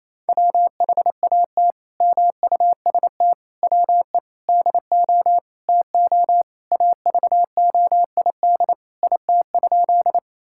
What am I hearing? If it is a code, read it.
Morse code, heard as a keyed tone, about 28 wpm, 700 Hz. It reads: WHAT MUST WE DO TO AVOID IT?
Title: Sentences from Top 1000 Words - Review Part 6 - 28wpm